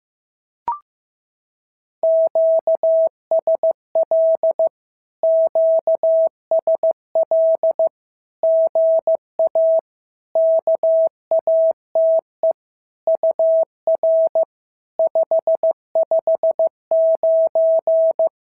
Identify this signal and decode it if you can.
Morse code, heard as a keyed tone, about 15 wpm, 650 Hz. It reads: QSL QSL GA KATE UR 559
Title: SOTA QSO Practice 15wpm